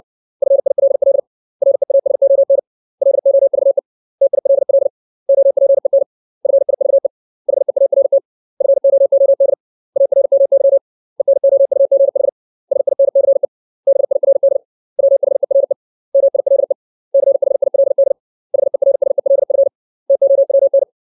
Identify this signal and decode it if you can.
Morse code, heard as a keyed tone, about 60 wpm, 550 Hz. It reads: FILL REASON LOVE TILL OPEN FIVE HEART FOOD ARMY ENOUGH SINCE BIRD WHERE MILE CHILD HERSELF TOWN
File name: 632 words / 632 Words 60WPM